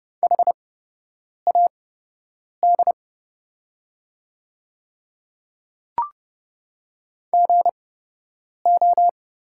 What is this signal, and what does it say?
Morse code, heard as a keyed tone, about 31 wpm, spaced as for 8 wpm, 700 Hz. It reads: HAD GO
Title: 8x Character Spacing - Top 200 Words - 30wpm